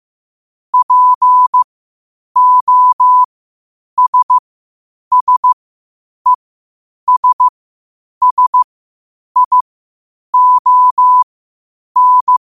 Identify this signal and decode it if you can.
Morse code, heard as a keyed tone, about 15 wpm, spaced as for 8 wpm, 1000 Hz. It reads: POSSESSION
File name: qotdfiles / 8wpm